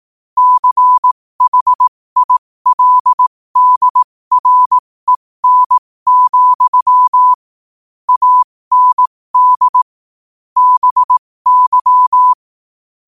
Morse code, heard as a keyed tone, about 18 wpm, spaced as for 15 wpm, 1000 Hz. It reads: CHILDREN, AND BY
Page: New quotes every day in morse code at 15 Words per minute.